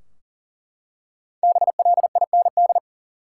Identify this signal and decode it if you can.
Morse code, heard as a keyed tone, about 40 wpm, 700 Hz. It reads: BLIND
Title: Five Letter Words - 40wpm